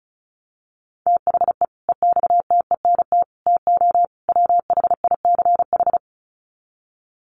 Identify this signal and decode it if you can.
Morse code, heard as a keyed tone, about 34 wpm, 700 Hz. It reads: THE EXTENT TO WHICH
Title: Sets of 4 Words 35wpm